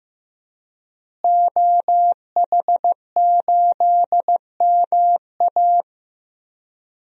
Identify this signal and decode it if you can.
Morse code, heard as a keyed tone, about 15 wpm, 700 Hz. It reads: OH8MA
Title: DXCC-100 Call Signs - 5 Characters - Repeated 3x - 15wpm